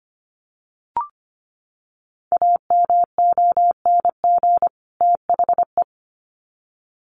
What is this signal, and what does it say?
Morse code, heard as a keyed tone, about 25 wpm, 700 Hz. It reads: AMONG THE
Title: Sets of 2 Words 25wpm